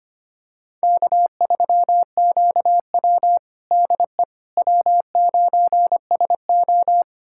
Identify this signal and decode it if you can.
Morse code, heard as a keyed tone, about 25 wpm, 700 Hz. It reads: K3QW DE W9SO
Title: USA Callsign DE Callsign - 4 Characters - 25wpm